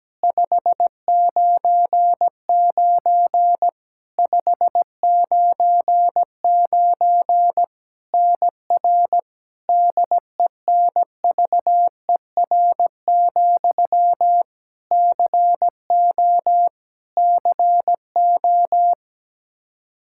Morse code, heard as a keyed tone, about 17 wpm, 700 Hz. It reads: 599 599 NR DENVER, CO CO
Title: Practice QSO - Exchange 1 - Beginning - 17wpm